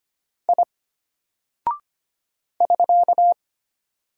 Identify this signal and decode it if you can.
Morse code, heard as a keyed tone, about 25 wpm, 700 Hz. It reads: I <SK>
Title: Single Character (Letter:Number:Punctuation:Prosign) Alt 25wpm